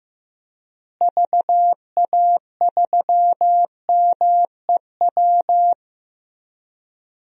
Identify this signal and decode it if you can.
Morse code, heard as a keyed tone, about 16 wpm, 700 Hz. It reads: VA3MEW
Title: DXCC-100 Call Signs - 6 Characters - Repeated 3x - 15wpm